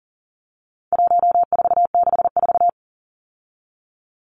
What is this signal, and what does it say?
Morse code, heard as a keyed tone, about 40 wpm, 700 Hz. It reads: 1464
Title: Numbers - 4-Digit 40wpm